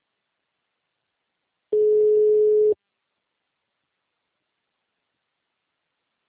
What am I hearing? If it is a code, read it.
Ringback tone (Europe, 425 Hz).